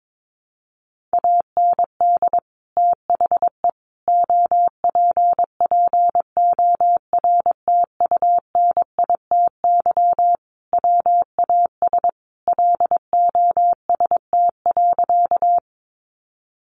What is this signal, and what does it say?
Morse code, heard as a keyed tone, about 22 wpm, 700 Hz. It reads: AND THE OPPORTUNITY WAS LOST.
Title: Sentences from Top 1400 Words 22wpm